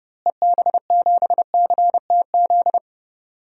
Morse code, heard as a keyed tone, about 32 wpm, 700 Hz. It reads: EB7CTZ
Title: DXCC-100 Call Signs - 6 Characters - Repeated 3x - 30wpm